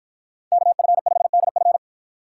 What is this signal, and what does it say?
Morse code, heard as a keyed tone, about 53 wpm, 700 Hz. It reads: KF5DF